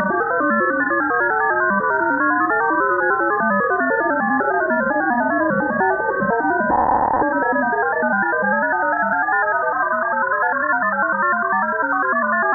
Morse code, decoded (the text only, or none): none